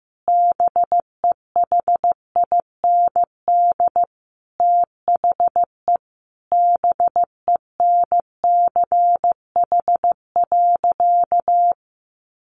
Morse code, heard as a keyed tone, about 15 wpm, 700 Hz. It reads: BEHIND THE BENCH.
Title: Sentences from Top 1400 Words 15wpm